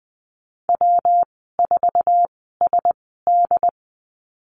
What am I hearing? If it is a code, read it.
Morse code, heard as a keyed tone, about 20 wpm, spaced as for 14 wpm, 700 Hz. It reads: W4SD
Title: 122 Lesson 20g - Number 3 - Call Signs 20wpm